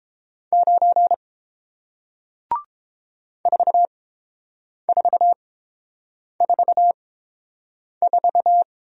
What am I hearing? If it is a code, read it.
Morse code, heard as a keyed tone, about 30 wpm, spaced as for 7 wpm, 700 Hz. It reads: 9 4444